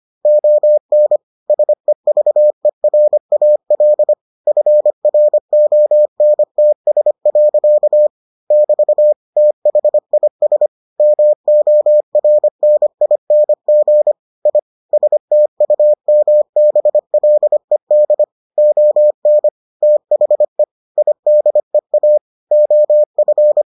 Morse code, heard as a keyed tone, about 25 wpm, 600 Hz. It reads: ON SEVERAL FRONTS. <BT> THIS MORNING I STUMBLED ON THE IDEA OF